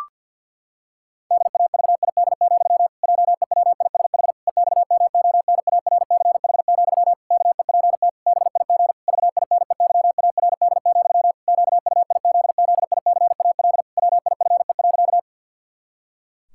Morse code, heard as a keyed tone, about 52 wpm, 700 Hz. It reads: DAVID, JEWISH EXMONARCH, KEPT BIG VINEYARD, QUIZZICAL WIFE.
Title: Pangram Challenge v2 - 50wpm